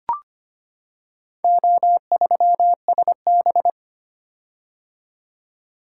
Morse code, heard as a keyed tone, about 25 wpm, 700 Hz. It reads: O3SB
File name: practice / 25wpm / Four Letter-Number Combinations - 25wpm